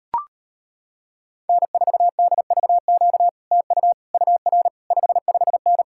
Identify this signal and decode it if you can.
Morse code, heard as a keyed tone, about 33 wpm, 700 Hz. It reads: N4DVQ TU UR 55N